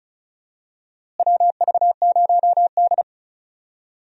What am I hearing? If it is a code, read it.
Morse code, heard as a keyed tone, about 36 wpm, 700 Hz. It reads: WV0D
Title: USA Call Signs - 4 Characters - Repeated 3x - 35wpm